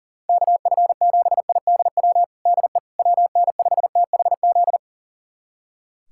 Morse code, heard as a keyed tone, about 40 wpm, 690 Hz. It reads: KF7IDW DE WN5THZ
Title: USA Callsign DE Callsign - 6 Characters - 40wpm